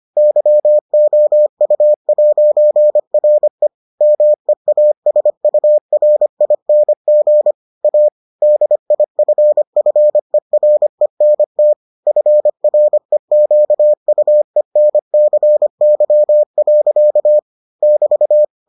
Morse code, heard as a keyed tone, about 25 wpm, 600 Hz. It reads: YOU'RE MEASURING A DIFFERENT FREQUENCY. <BT>